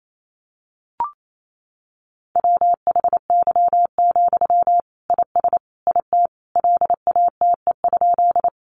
Morse code, heard as a keyed tone, about 28 wpm, 700 Hz. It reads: WHY, IS IT LATE?
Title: Sentences from Top 300 Words 28wpm